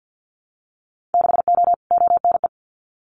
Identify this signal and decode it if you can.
Morse code, heard as a keyed tone, about 50 wpm, 700 Hz. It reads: 6O ONE
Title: Winter Field Day Exchanges Encore 50wpm